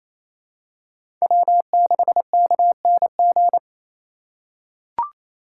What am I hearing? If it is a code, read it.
Morse code, heard as a keyed tone, about 28 wpm, 700 Hz. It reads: W6KNG
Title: USA Call Signs - 3x Character Spacing - 28wpm